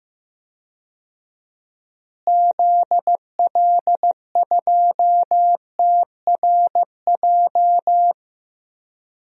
Morse code, heard as a keyed tone, about 15 wpm, 700 Hz. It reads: ZL2TRJ
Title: DXCC-100 Call Signs - 6 Characters - Repeated 2x - 15wpm